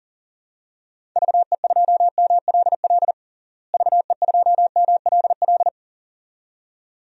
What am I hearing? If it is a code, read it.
Morse code, heard as a keyed tone, about 40 wpm, 700 Hz. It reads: VE2MLL VE2MLL
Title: DXCC-100 Call Signs - 6 Characters - Repeated 2x - 40wpm